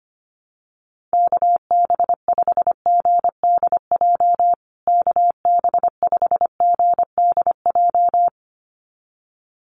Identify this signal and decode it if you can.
Morse code, heard as a keyed tone, about 25 wpm, 700 Hz. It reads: KB5GDJ KB5GDJ
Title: USA Call Signs Repeated 2x 25wpm